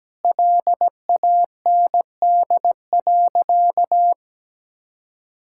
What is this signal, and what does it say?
Morse code, heard as a keyed tone, about 17 wpm, 700 Hz. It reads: LAND.
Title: Sentences from Top 1600 Words 17wpm